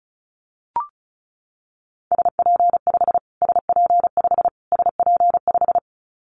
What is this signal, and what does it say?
Morse code, heard as a keyed tone, about 34 wpm, 700 Hz. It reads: SP5 SP5 SP5
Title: Callsign Prefixes for Top 25 DXCC - Repeated 3x - 35wpm